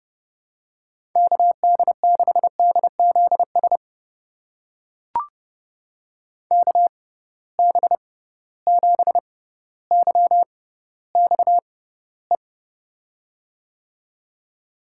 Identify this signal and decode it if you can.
Morse code, heard as a keyed tone, about 32 wpm, spaced as for 16 wpm, 700 Hz. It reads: KD6DZS KB7YXE
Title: USA Call Signs - 6 Characters - 6x Character Spacing - 30wpm